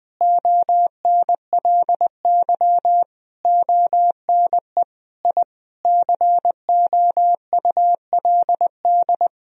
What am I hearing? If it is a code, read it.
Morse code, heard as a keyed tone, about 20 wpm, 700 Hz. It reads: ONLY ONE I COULD
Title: Sentences from Top 1000 Words - Review Part 4 - 20wpm